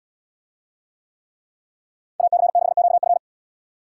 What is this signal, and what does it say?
Morse code, heard as a keyed tone, about 72 wpm, 700 Hz. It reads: AC6QF